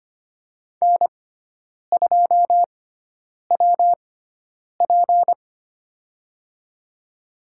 Morse code, heard as a keyed tone, about 25 wpm, spaced as for 9 wpm, 700 Hz. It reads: N2WP
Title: USA Call Signs - 6x Character Spacing - 25wpm